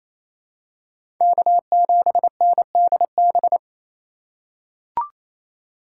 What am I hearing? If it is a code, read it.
Morse code, heard as a keyed tone, about 28 wpm, 700 Hz. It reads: K7NDB
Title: USA Call Signs - 5 Characters - 4x Character Spacing - 28wpm